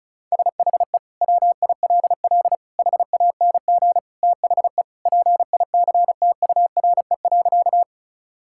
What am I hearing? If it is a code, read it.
Morse code, heard as a keyed tone, about 36 wpm, 700 Hz. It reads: SHE WILL HANG THE PICTURE.
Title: Sentences from Top 1600 Words 35wpm